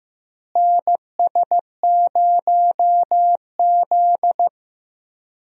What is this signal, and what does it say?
Morse code, heard as a keyed tone, about 15 wpm, 700 Hz. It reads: NS0Z
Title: USA Call Signs - 4 Characters - Repeated 3x - 15wpm